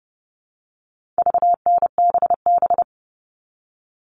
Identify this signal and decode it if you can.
Morse code, heard as a keyed tone, about 30 wpm, 700 Hz. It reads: VNBB
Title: All Cut Numbers - 4-Digit 30wpm